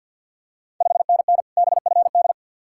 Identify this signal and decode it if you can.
Morse code, heard as a keyed tone, about 52 wpm, 700 Hz. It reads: 5NN BFD
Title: Ontario QSO Party Exchange 50wpm